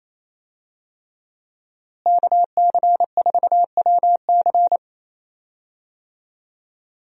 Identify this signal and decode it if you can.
Morse code, heard as a keyed tone, about 27 wpm, 700 Hz. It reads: KC4WC
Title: USA Call Signs - 5 Characters - Repeated 3x - 28wpm